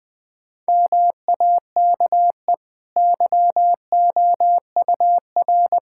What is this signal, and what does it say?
Morse code, heard as a keyed tone, about 21 wpm, 700 Hz. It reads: MAKE YOUR
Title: Easy Sentences with Hints - 20wpm